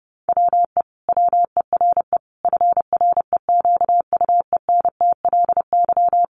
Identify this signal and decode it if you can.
Morse code, heard as a keyed tone, about 30 wpm, 700 Hz. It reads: WE WERE FREQUENTLY
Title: Sentences from Top 1300 Words 30wpm